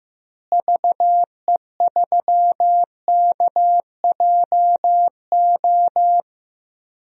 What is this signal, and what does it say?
Morse code, heard as a keyed tone, about 15 wpm, 700 Hz. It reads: VE3KJO
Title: DXCC-100 Call Signs - 6 Characters - Repeated 2x - 15wpm